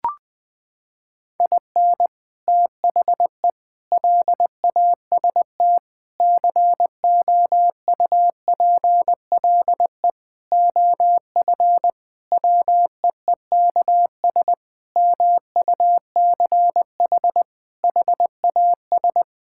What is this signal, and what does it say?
Morse code, heard as a keyed tone, about 21 wpm, 700 Hz. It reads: IN THE LAST COUPLE OF WEEKS MUCH HAS
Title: Sentences with 5-Word Phrases - 20wpm